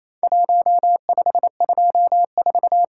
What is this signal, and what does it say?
Morse code, heard as a keyed tone, about 28 wpm, 700 Hz. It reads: 1524
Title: Worked All Europe DX Contest - QTC Exchange - 28wpm